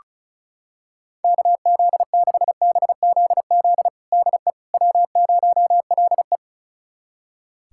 Morse code, heard as a keyed tone, about 36 wpm, 700 Hz. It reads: KZ6BZZ DE W0LE
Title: USA Callsign DE Callsign 35wpm